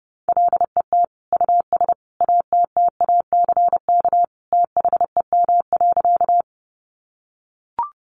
Morse code, heard as a keyed tone, about 30 wpm, 700 Hz. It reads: LET US ATTACK THEM.
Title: Sentences from Top 1500 Words 30wpm